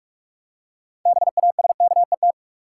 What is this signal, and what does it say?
Morse code, heard as a keyed tone, about 46 wpm, 700 Hz. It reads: BASKET
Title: Six Letter Words - 45wpm